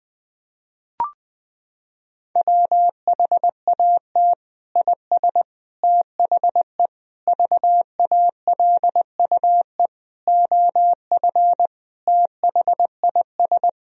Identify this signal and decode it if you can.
Morse code, heard as a keyed tone, about 20 wpm, 700 Hz. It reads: WHAT IS THE VALUE OF THIS
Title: Sentences from Top 1000 Words - Review Part 3 - 20wpm